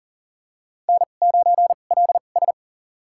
Morse code, heard as a keyed tone, about 40 wpm, spaced as for 29 wpm, 700 Hz. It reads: N9LS
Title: USA Call Signs - 2x Character Spacing - 40wpm